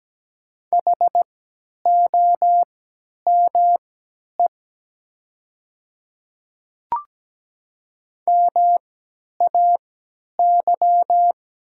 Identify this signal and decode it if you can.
Morse code, heard as a keyed tone, about 17 wpm, spaced as for 10 wpm, 690 Hz. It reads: HOME MAY
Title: 3x Character Spacing - Top 500 Words - 17wpm